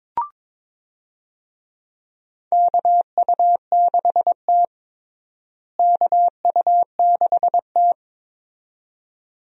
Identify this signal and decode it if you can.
Morse code, heard as a keyed tone, about 22 wpm, 700 Hz. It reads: KU6T KU6T
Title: USA Call Signs - 4 Characters - Repeated 2x - 22wpm